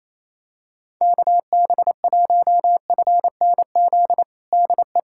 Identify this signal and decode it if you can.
Morse code, heard as a keyed tone, about 28 wpm, 700 Hz. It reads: KB1FNZ DE
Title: USA Callsign DE Callsign - 6 Characters - 28wpm